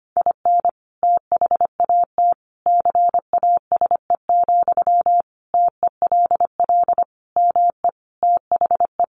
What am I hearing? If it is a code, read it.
Morse code, heard as a keyed tone, about 25 wpm, 700 Hz. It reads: IN THAT CASE, TELL ME THE
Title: Sentences from Top 1400 Words 25wpm